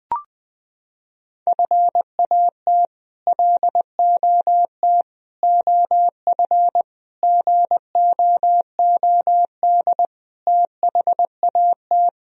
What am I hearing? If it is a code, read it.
Morse code, heard as a keyed tone, about 20 wpm, 700 Hz. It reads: FAT LOT OF GOOD THAT
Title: Sentences from Top 1000 Words - Review Part 4 - 20wpm